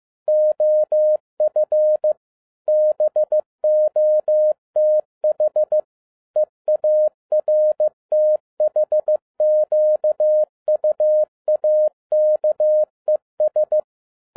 Morse code, heard as a keyed tone, about 15 wpm, 610 Hz. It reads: OF BOTH EARTHQUAKES
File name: 2011-10-mp3 / n02-FIELD DAY CONTEST QRV FROM SAKAE VILLAGE 15wpm